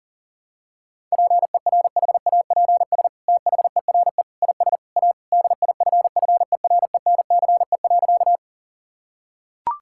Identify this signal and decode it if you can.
Morse code, heard as a keyed tone, about 40 wpm, 700 Hz. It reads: PERHAPS THERE IS A DIFFERENCE.
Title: Sentences from Top 1000 Words - Review Part 5 - 40wpm